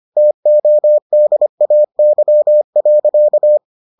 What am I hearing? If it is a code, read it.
Morse code, heard as a keyed tone, about 25 wpm, 600 Hz. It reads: TODAY.